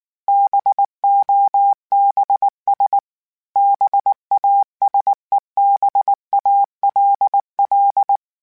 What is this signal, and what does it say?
Morse code, heard as a keyed tone, about 19 wpm, 800 Hz. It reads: BOBS BASEBALL
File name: Misc / Misc-mp3s / BIG @18